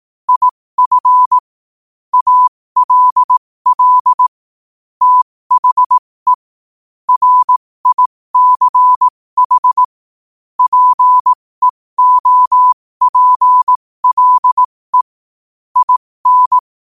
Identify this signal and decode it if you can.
Morse code, heard as a keyed tone, about 18 wpm, spaced as for 15 wpm, 1000 Hz. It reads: IF ALL THE RICH PEOPLE IN
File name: qotdfiles / 15wpm